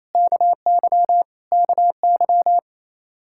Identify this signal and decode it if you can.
Morse code, heard as a keyed tone, about 28 wpm, 700 Hz. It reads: KY KY
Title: Practice QSO - Exchange 1 - Beginning - 28wpm